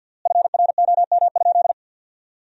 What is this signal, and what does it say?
Morse code, heard as a keyed tone, about 50 wpm, 700 Hz. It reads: FROM?
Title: 3x Word-Spacing - Sentences from Top 300 Words - 50wpm